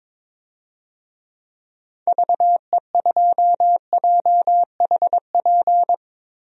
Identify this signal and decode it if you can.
Morse code, heard as a keyed tone, about 22 wpm, 700 Hz. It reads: VE2JHP
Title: DXCC-100 Call Signs - 6 Characters - 22wpm